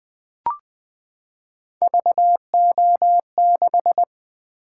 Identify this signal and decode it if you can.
Morse code, heard as a keyed tone, about 20 wpm, 700 Hz. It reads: VO6
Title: Three Letter-Number Combinations - 20wpm